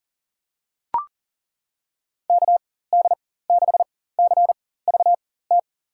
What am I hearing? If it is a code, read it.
Morse code, heard as a keyed tone, about 37 wpm, spaced as for 18 wpm, 700 Hz. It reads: KD6CVT
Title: USA Call Signs - 6 Characters - 4x Character Spacing - 40wpm